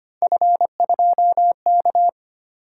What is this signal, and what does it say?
Morse code, heard as a keyed tone, about 25 wpm, 700 Hz. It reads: F2K
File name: Three Letter-Number Combinations - 25wpm